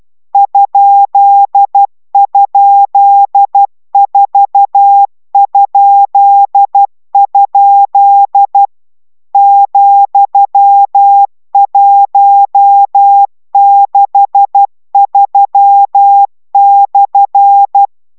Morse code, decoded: ??4?? ,163/

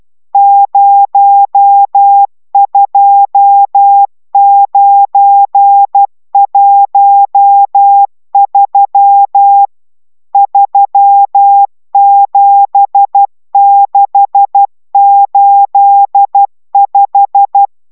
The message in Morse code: 02913 37685

